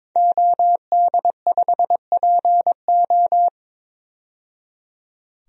Morse code, heard as a keyed tone, about 23 wpm, 700 Hz. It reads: OD5PO
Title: DXCC-100 Call Signs - 5 Characters - 22wpm